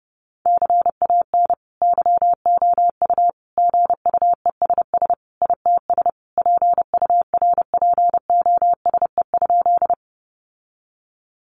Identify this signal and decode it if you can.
Morse code, heard as a keyed tone, about 28 wpm, 700 Hz. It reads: CAN YOU GUESS ITS PURPOSE?
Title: Sentences from Top 1500 Words 30wpm